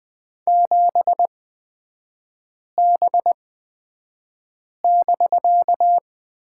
Morse code, heard as a keyed tone, about 20 wpm, spaced as for 5 wpm, 700 Hz. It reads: 7B<BK>
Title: 250 Lesson 40e - BK - Three Characters 20wpm